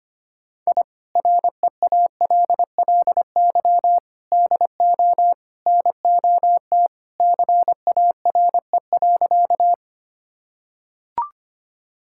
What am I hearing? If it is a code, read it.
Morse code, heard as a keyed tone, about 25 wpm, 700 Hz. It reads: I REALLY DO NOT CARE.
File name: practice / 25wpm / Sentences with 4-Word Phrases - 25wpm